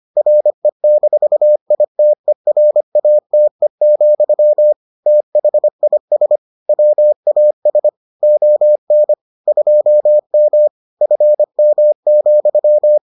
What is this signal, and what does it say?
Morse code, heard as a keyed tone, about 25 wpm, 600 Hz. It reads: RE-ITERATE, THIS WAS ON 2M FM,